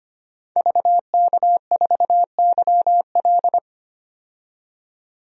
Morse code, heard as a keyed tone, about 25 wpm, 700 Hz. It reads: VK4YL
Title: DXCC-100 Call Signs - 5 Characters - 25wpm